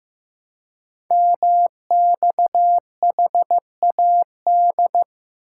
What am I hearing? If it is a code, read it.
Morse code, heard as a keyed tone, about 15 wpm, 700 Hz. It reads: MXHAD
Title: Code Groups - 5 Letters - 15wpm